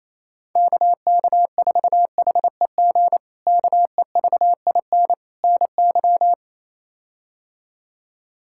Morse code, heard as a keyed tone, about 28 wpm, 700 Hz. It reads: KK4HEG KEVIN NY
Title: K1USN SST QSO Practice - 28wpm